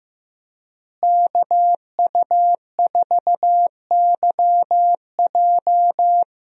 Morse code, heard as a keyed tone, about 15 wpm, 700 Hz. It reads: KU4YJ
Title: USA Call Signs - 5 Characters - Repeated 2x - 15wpm